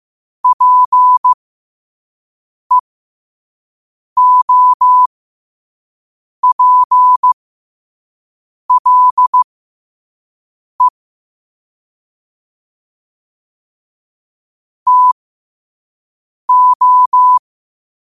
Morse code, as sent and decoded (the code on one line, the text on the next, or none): .--. --- .--. .-.. / - ---
PEOPLE TO